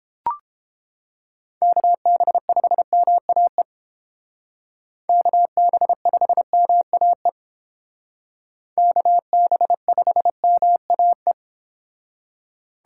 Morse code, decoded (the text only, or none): KB5MAE KB5MAE KB5MAE